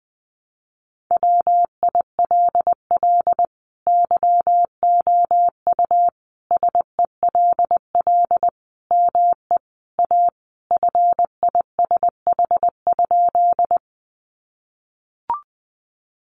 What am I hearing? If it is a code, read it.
Morse code, heard as a keyed tone, about 20 wpm, 700 Hz. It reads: WILL YOU SELL ME A FISH?
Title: Sentences from Top 1100 Words 20wpm